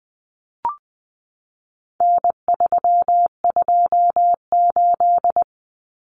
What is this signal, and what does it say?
Morse code, heard as a keyed tone, about 20 wpm, 700 Hz. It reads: N328
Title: Common Cut Numbers - 4-Digit 20wpm